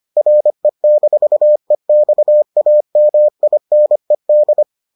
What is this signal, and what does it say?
Morse code, heard as a keyed tone, about 25 wpm, 600 Hz. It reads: RE-EXAMINED